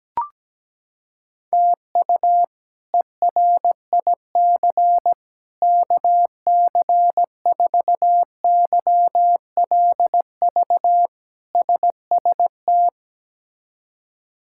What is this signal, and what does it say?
Morse code, heard as a keyed tone, about 17 wpm, 700 Hz. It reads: TU ERIC KC4YLV SST
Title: K1USN SST QSO Practice - 17wpm